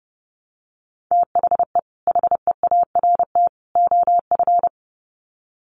Morse code, heard as a keyed tone, about 30 wpm, 700 Hz. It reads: THE HEART OF
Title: Sets of 3 Words 30wpm